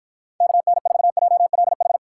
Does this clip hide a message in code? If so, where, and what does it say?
Yes: Morse code from 0.4 to 2.0 s, heard as a keyed tone, about 53 wpm, 700 Hz. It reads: KN4JLH